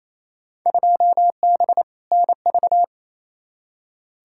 Morse code, heard as a keyed tone, about 28 wpm, 700 Hz. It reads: 2B NV
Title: Field Day Exchanges Encore - 28wpm